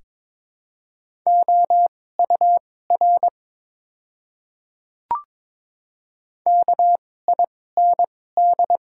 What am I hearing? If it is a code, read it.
Morse code, heard as a keyed tone, about 22 wpm, spaced as for 16 wpm, 700 Hz. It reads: OUR KIND
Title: 2x Character Spacing - Top 500 Words - 22wpm